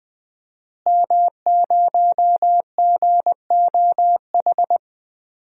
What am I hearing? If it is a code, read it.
Morse code, heard as a keyed tone, about 20 wpm, 700 Hz. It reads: M0GOH
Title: DXCC-100 Call Signs - 5 Characters - Repeated 2x - 20wpm